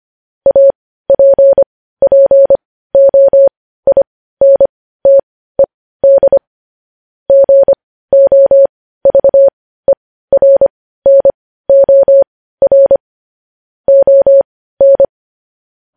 Morse code, heard as a keyed tone, about 25 wpm, spaced as for 15 wpm, 550 Hz. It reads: APPOINTED GOVERNOR ON
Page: news-headlines-in-morse-code-at-15-wpm